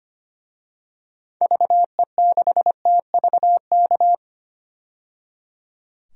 Morse code, heard as a keyed tone, about 25 wpm, 700 Hz. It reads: VE6TVK
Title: DXCC-100 Call Signs - 6 Characters - 25wpm